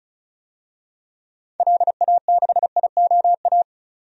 Morse code, heard as a keyed tone, about 36 wpm, 700 Hz. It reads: LA6IOA
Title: DXCC-100 Call Signs - 6 Characters - 35wpm